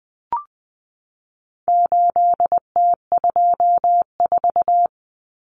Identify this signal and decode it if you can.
Morse code, heard as a keyed tone, about 20 wpm, 700 Hz. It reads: 8T24
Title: Common Cut Numbers - 4-Digit 20wpm